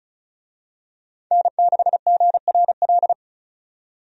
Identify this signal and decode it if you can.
Morse code, heard as a keyed tone, about 36 wpm, 700 Hz. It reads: N6GRL
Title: USA Call Signs - 5 Characters - 35wpm